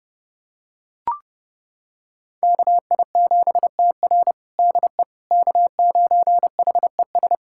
Morse code, heard as a keyed tone, about 30 wpm, 700 Hz. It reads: KI7TR DE K9HES
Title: USA Callsign DE Callsign - 5 Characters - 30wpm